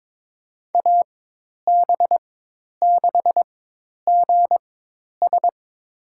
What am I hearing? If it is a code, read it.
Morse code, heard as a keyed tone, about 22 wpm, spaced as for 10 wpm, 700 Hz. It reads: AB6GS
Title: USA Call Signs - 5 Characters - 4x Character Spacing - 22wpm